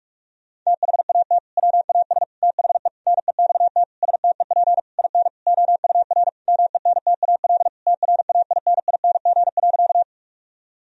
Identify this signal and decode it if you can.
Morse code, heard as a keyed tone, about 42 wpm, 700 Hz. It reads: THAT WAS THE NEXT STEP IN OUR MENTAL TRAINING.
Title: Sentences from Top 1000 Words - Review Part 4 - 45wpm